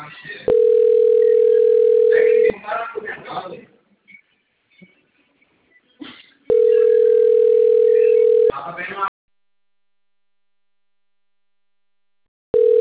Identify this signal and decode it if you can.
Ringback tone (North America), 440 + 480 Hz, with speech.
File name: rg-1600-unknown- - - .80179